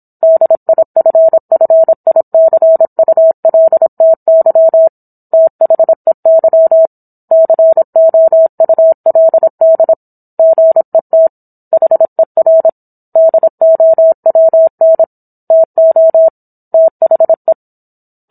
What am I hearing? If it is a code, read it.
Morse code, heard as a keyed tone, about 27 wpm, 650 Hz. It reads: DIFFICULTY THEY COULD GET HER DOWN TO THE